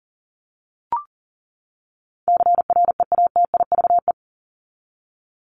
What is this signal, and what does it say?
Morse code, heard as a keyed tone, about 40 wpm, 700 Hz. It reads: CREATIVE
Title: Top 751-1000 Words 40wpm